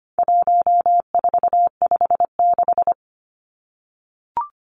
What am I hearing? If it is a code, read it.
Morse code, heard as a keyed tone, about 25 wpm, 700 Hz. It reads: 1456
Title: Numbers - 4-Digit 25wpm